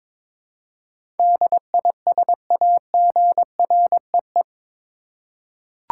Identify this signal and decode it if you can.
Morse code, heard as a keyed tone, about 22 wpm, 700 Hz. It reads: DISAGREE
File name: practice / 22wpm / Eight Letter Words - 22wpm